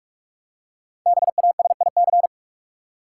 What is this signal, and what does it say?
Morse code, heard as a keyed tone, about 46 wpm, 700 Hz. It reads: BASIC
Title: Five Letter Words - 45wpm